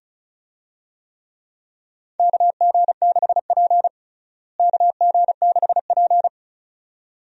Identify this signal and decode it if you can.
Morse code, heard as a keyed tone, about 34 wpm, 700 Hz. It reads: KG6P KG6P
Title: USA Call Signs - 4 Characters - Repeated 2x - 35wpm